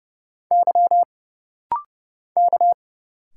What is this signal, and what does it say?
Morse code, heard as a keyed tone, about 31 wpm, 700 Hz. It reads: Y K
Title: Single Letters - Rapid-Fire 30wpm